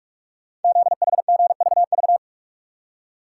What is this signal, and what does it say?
Morse code, heard as a keyed tone, about 44 wpm, 690 Hz. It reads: ZHGVV
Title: Code Groups - 5 Letters - 45wpm